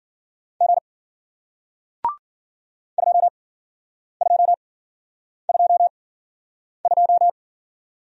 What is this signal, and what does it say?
Morse code, heard as a keyed tone, about 53 wpm, spaced as for 9 wpm, 700 Hz. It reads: G 2222